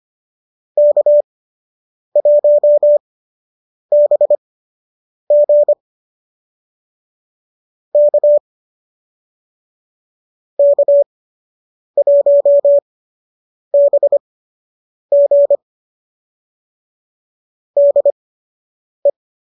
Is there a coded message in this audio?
Yes: Morse code from 0.8 to 19.1 s, heard as a keyed tone, about 25 wpm, spaced as for 8 wpm, 570 Hz. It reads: K1BG K K1BG DE